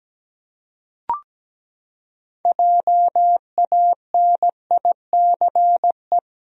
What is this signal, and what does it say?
Morse code, heard as a keyed tone, about 17 wpm, 700 Hz. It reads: JANICE
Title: USA Common Names - 17wpm